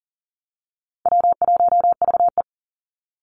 Morse code, heard as a keyed tone, about 40 wpm, 700 Hz. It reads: W1VE
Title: Easy Call Signs Repeated 2x - 40wpm